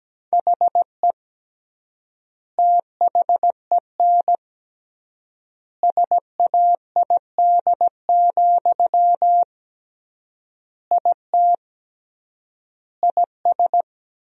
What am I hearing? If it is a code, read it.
Morse code, heard as a keyed tone, about 17 wpm, 700 Hz. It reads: HE THEN SAID, IT IS
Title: 3x Word-Spacing - Sentences from Top 300 Words - 17wpm